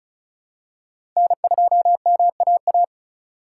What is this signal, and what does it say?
Morse code, heard as a keyed tone, about 34 wpm, 700 Hz. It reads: N2MAA
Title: USA Call Signs - 2x Character Spacing - 35wpm